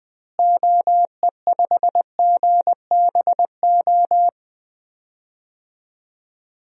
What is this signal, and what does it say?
Morse code, heard as a keyed tone, about 21 wpm, 700 Hz. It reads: OE5GBO
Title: DXCC-100 Call Signs - 6 Characters - Repeated 2x - 20wpm